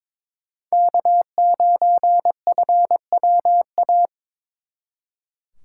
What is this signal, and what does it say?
Morse code, heard as a keyed tone, about 27 wpm, 700 Hz. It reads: K9FWA
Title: Speed Racing - USA 1x3 Call Signs 22wpm